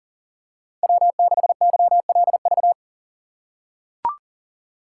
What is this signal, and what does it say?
Morse code, heard as a keyed tone, about 37 wpm, 700 Hz. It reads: W6YLV
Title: USA Call Signs - 5 Characters - 2x Character Spacing - 40wpm